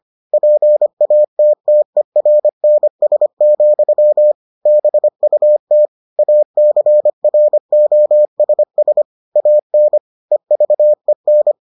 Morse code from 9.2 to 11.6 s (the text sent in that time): AN EVEN